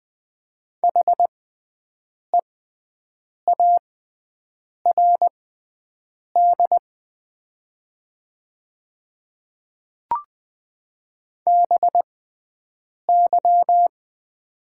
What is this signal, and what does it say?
Morse code, heard as a keyed tone, about 20 wpm, spaced as for 7 wpm, 700 Hz. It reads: HEARD BY
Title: 6x Character Spacing - Top 500 Words - 20wpm